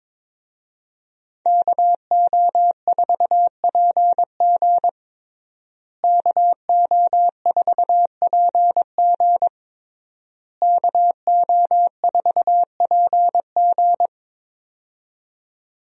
Morse code, heard as a keyed tone, about 22 wpm, 700 Hz. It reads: KO4PG KO4PG KO4PG
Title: USA Call Signs - 5 Characters - Repeated 3x - 22wpm